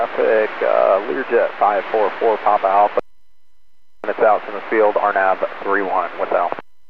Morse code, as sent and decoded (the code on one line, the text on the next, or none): none